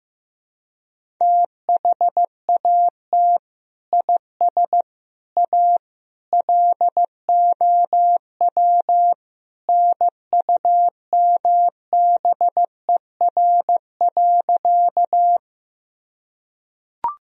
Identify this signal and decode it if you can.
Morse code, heard as a keyed tone, about 15 wpm, 700 Hz. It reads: THAT IS A LOW NUMBER.
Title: Sentences from Top 1000 Words - Review Part 1 - 15wpm